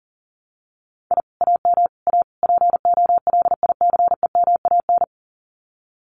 Morse code, heard as a keyed tone, about 40 wpm, 700 Hz. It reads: I AM A POLICEMAN
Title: Common QSO Phrases Encore 40wpm